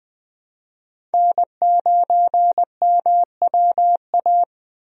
Morse code, heard as a keyed tone, about 20 wpm, 700 Hz. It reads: N9MWA
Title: USA Call Signs - 5 Characters - Repeated 3x - 20wpm